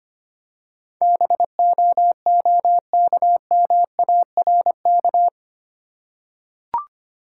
Morse code, heard as a keyed tone, about 25 wpm, 700 Hz. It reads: BOOKMARK
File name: practice / 25wpm / Easy Words with Hints - 25wpm